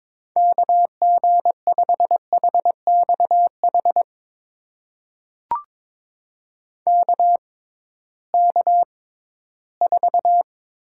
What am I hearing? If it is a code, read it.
Morse code, heard as a keyed tone, about 22 wpm, spaced as for 14 wpm, 700 Hz. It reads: KG5HXH KK4
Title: USA Call Signs - 6x Character Spacing - 22wpm